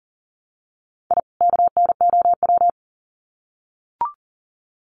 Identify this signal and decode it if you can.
Morse code, heard as a keyed tone, about 40 wpm, 700 Hz. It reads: I KNOW
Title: Sets of 2 Words 40wpm